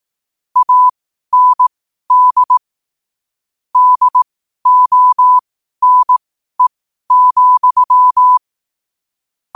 Morse code, heard as a keyed tone, about 18 wpm, spaced as for 12 wpm, 1000 Hz. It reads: AND DONE,
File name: qotdfiles / 12wpm